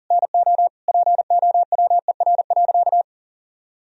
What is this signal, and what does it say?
Morse code, heard as a keyed tone, about 40 wpm, 700 Hz. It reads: NO POWER.
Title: Sentences from Top 1000 Words - Review Part 2 - 40wpm